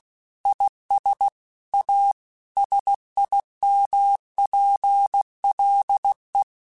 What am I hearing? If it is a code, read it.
Morse code, heard as a keyed tone, about 16 wpm, 800 Hz. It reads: IS A SIMPLE
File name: Technical Subjects / Technical Subjects-mp3s / Sw Pwr Sup @15